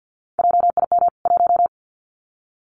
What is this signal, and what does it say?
Morse code, heard as a keyed tone, about 50 wpm, 700 Hz. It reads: JIM 1
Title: CWT Exchange 50wpm